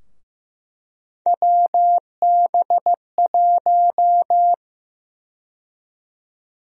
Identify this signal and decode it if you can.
Morse code, heard as a keyed tone, about 15 wpm, 700 Hz. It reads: WB1
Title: USA Call Sign Prefixes - Repeated 2x - 15wpm